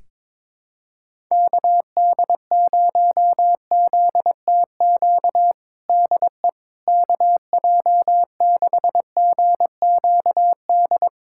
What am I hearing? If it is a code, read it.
Morse code, heard as a keyed tone, about 22 wpm, 700 Hz. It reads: KD0ZTQ DE KJ6GQD
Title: USA Callsign DE Callsign 22wpm